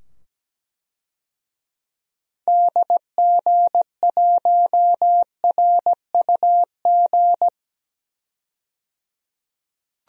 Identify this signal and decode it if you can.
Morse code, heard as a keyed tone, about 17 wpm, 700 Hz. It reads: DG1RUG
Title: DXCC-100 Call Signs - 6 Characters - 17wpm